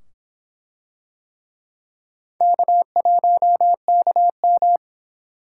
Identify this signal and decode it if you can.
Morse code, heard as a keyed tone, about 24 wpm, 700 Hz. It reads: K1KM